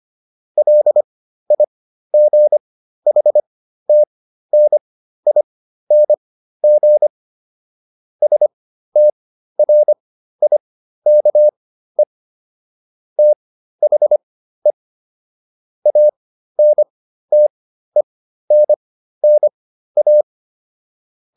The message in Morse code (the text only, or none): LIGHTNING STRIKE THE ANTENNA